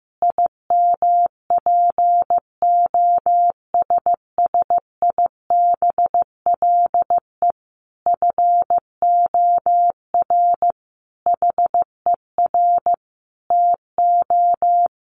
Morse code, heard as a keyed tone, about 15 wpm, 700 Hz. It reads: IMPOSSIBLE FOR HER TO
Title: Sentences from Top 1100 Words 15wpm